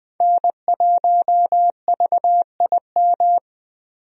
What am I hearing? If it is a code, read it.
Morse code, heard as a keyed tone, about 20 wpm, 700 Hz. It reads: N1VIM
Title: USA Call Signs - 5 Characters - 20wpm